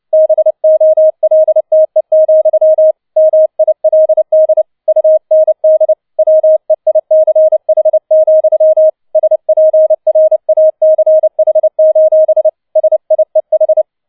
Morse code, heard as a keyed tone, about 29 wpm, 630 Hz. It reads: BOLTE, MILD UND WEICH, SPRACH: SIEH
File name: 007 Letzter Streich 30wpm